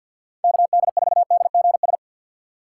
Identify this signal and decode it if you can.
Morse code, heard as a keyed tone, about 51 wpm, 700 Hz. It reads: KD4DGS
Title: USA Call Signs - 6 Characters - 4x Character Spacing - 50wpm